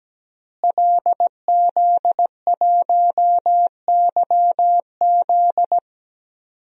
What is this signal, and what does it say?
Morse code, heard as a keyed tone, about 17 wpm, 700 Hz. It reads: LZ1YZ
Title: DXCC-100 Call Signs - 5 Characters - Repeated 2x - 17wpm